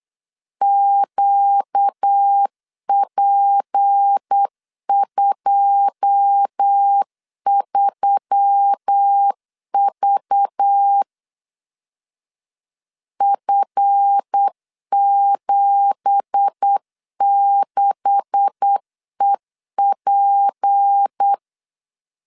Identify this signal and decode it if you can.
Morse code, heard as a keyed tone, about 8 wpm, 800 Hz. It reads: QP23V F76EP
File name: 8wpm